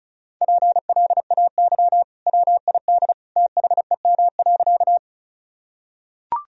Morse code, heard as a keyed tone, about 36 wpm, 700 Hz. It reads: PLAY WID THEM.
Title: Sentences from Top 200 Words and Abbreviations 35wpm